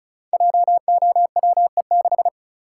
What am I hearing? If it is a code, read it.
Morse code, heard as a keyed tone, about 36 wpm, 700 Hz. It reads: JOWE6
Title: Code Groups - 5 Letter:Numbers - 35wpm